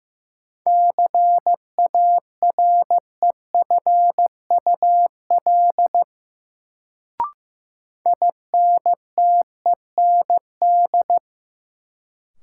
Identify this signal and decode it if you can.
Morse code, heard as a keyed tone, about 16 wpm, 700 Hz. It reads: CAREFUL INTEND
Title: Sentences from Top 1600 Words - New Word Drill - 15wpm